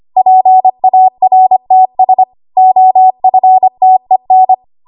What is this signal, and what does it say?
Morse code, heard as a keyed tone, about 25 wpm, 750 Hz. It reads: PARTS OFTEN
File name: Archive / 25 WPM / 25WPM